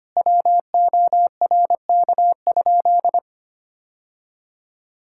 Morse code, heard as a keyed tone, about 25 wpm, 700 Hz. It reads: WORK?
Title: Sentences from Top 1000 Words - Review Part 6 - 25wpm